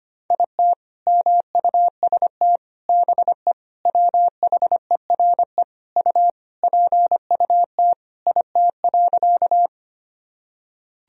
Morse code, heard as a keyed tone, about 25 wpm, 700 Hz. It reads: IT MUST BE WHERE U PUT IT.
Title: Sentences from Top 200 Words and Abbreviations 25wpm